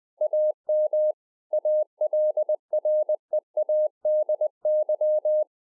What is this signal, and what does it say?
Morse code, heard as a keyed tone, about 20 wpm, 610 Hz. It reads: AM ALREADY
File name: 20WPM